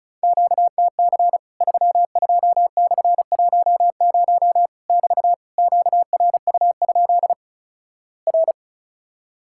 Morse code, heard as a keyed tone, about 36 wpm, 700 Hz. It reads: QTC 32/10 <BT> QRU? R
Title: Worked All Europe DX Contest - QTC Exchange - 35wpm